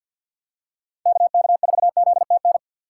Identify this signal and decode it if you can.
Morse code, heard as a keyed tone, about 52 wpm, 700 Hz. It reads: KK4ZTN
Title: USA Call Signs - 6 Characters - 3x Character Spacing - 50wpm